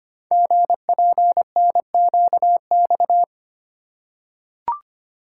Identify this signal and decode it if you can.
Morse code, heard as a keyed tone, about 25 wpm, 700 Hz. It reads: GPNQX
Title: Code Groups - 5 Letter:Numbers - 25wpm